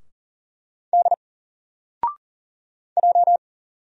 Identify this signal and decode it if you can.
Morse code, heard as a keyed tone, about 40 wpm, 700 Hz. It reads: D J
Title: Single Letter-Number Alt 40wpm